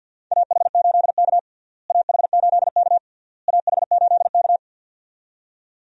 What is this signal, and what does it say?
Morse code, heard as a keyed tone, about 50 wpm, 700 Hz. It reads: AH8K AH8K AH8K
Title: USA Call Signs - 4 Characters - Repeated 3x - 50wpm